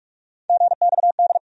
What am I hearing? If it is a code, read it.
Morse code, heard as a keyed tone, about 43 wpm, 700 Hz. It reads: GXD
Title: USA Call Sign Postfixes - 3 Letters 45wpm